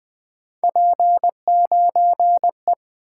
Morse code, heard as a keyed tone, about 20 wpm, 700 Hz. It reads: P9E
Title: Three Letter-Number Combinations - 20wpm